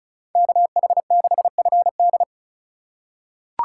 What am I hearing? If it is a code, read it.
Morse code, heard as a keyed tone, about 36 wpm, 700 Hz. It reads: KH6FD
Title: USA Call Signs - 4x Character Spacing - 35wpm